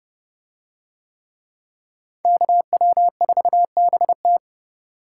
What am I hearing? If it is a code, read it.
Morse code, heard as a keyed tone, about 30 wpm, 700 Hz. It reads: KW4BT